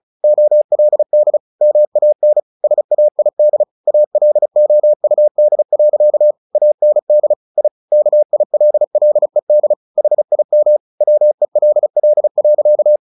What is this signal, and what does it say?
Morse code, heard as a keyed tone, about 34 wpm, 600 Hz. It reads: OLD MAN SAID ALOUD. AND I KILLED HIM WELL.
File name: the old man and the sea 2 35wpm